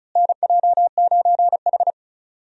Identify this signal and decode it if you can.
Morse code, heard as a keyed tone, about 36 wpm, 700 Hz. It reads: NJ9H
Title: USA Call Signs - 4 Characters - Repeated 3x - 35wpm